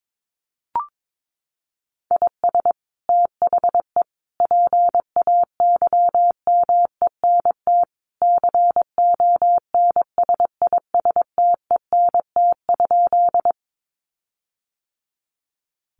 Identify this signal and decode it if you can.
Morse code, heard as a keyed tone, about 22 wpm, 700 Hz. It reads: IS THE PAYMENT CONSISTENT?
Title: Sentences from Top 1500 Words 22wpm